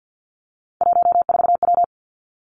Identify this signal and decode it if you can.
Morse code, heard as a keyed tone, about 50 wpm, 700 Hz. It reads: 14W
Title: Three Challenging Letter-Number Combinations 50wpm